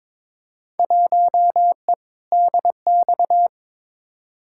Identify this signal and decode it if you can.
Morse code, heard as a keyed tone, about 22 wpm, 700 Hz. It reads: 1E DX
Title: Field Day Exchanges Encore - 22wpm